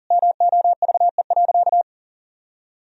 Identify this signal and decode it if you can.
Morse code, heard as a keyed tone, about 40 wpm, 700 Hz. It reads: MOVE.
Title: Sentences from Top 1000 Words - Review Part 1 - 40wpm